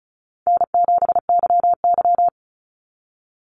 Morse code, heard as a keyed tone, about 34 wpm, 700 Hz. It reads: N7YY
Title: NA SOTA Chaser Callsigns 35wpm